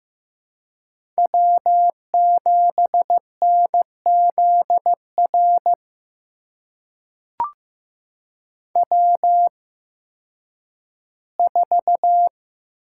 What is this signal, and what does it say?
Morse code, heard as a keyed tone, about 15 wpm, spaced as for 10 wpm, 700 Hz. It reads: W7NZR W4
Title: USA Call Signs - 5 Characters - 8x Character Spacing - 15wpm